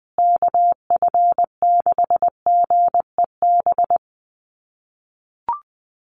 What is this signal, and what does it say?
Morse code, heard as a keyed tone, about 20 wpm, 700 Hz. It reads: KF6GEB
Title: USA Call Signs Repeated 3x 20wpm